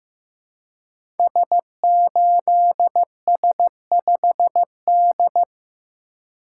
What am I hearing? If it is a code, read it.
Morse code, heard as a keyed tone, about 15 wpm, 700 Hz. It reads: S8S5D
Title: Code Groups - 5 Letter:Numbers - 15wpm